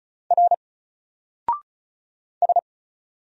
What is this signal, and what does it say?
Morse code, heard as a keyed tone, about 33 wpm, 700 Hz. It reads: R S
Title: Single Character (Letter:Number:Punctuation:Prosign) 35wpm